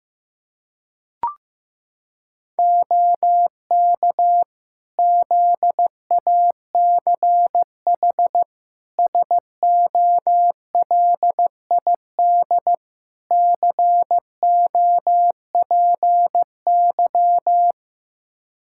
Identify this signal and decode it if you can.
Morse code, heard as a keyed tone, about 15 wpm, 700 Hz. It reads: OK ZACH SOLID COPY
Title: K1USN SST QSO Practice 15wpm